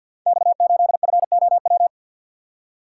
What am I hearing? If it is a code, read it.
Morse code, heard as a keyed tone, about 46 wpm, 700 Hz. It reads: X8FOW
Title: Code Groups - 5 Letter:Numbers - 50wpm